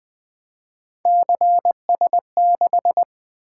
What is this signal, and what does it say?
Morse code, heard as a keyed tone, about 20 wpm, 700 Hz. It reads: CS6
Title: Three Letter-Number Combinations - 20wpm